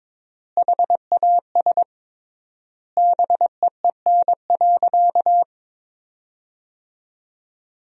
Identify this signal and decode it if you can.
Morse code, heard as a keyed tone, about 21 wpm, 700 Hz. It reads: HAS BEEN.
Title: 3x Word-Spacing - Sentences from Top 300 Words - 22wpm